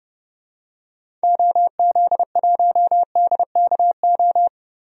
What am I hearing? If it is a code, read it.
Morse code, heard as a keyed tone, about 29 wpm, 690 Hz. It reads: OZ1DKO
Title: DXCC-100 Call Signs - 6 Characters - 30wpm